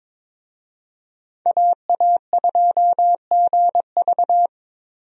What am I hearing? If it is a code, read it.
Morse code, heard as a keyed tone, about 22 wpm, 700 Hz. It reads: AA2GV
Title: USA Call Signs - 5 Characters - 22wpm